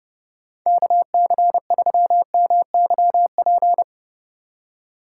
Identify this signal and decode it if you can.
Morse code, heard as a keyed tone, about 36 wpm, 700 Hz. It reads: KC3MYP
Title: Speed Racing - USA 2x3 Call Signs 30wpm